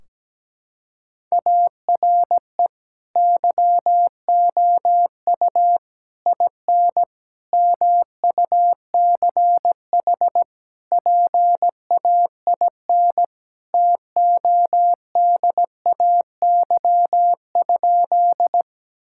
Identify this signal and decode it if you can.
Morse code, heard as a keyed tone, about 17 wpm, 700 Hz. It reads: ARE YOU IN MUCH PAIN TODAY?
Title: Sentences from Top 1600 Words 17wpm